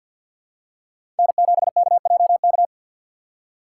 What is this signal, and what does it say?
Morse code, heard as a keyed tone, about 53 wpm, 700 Hz. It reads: N7KJK
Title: USA Call Signs - 5 Characters - 6x Character Spacing - 50wpm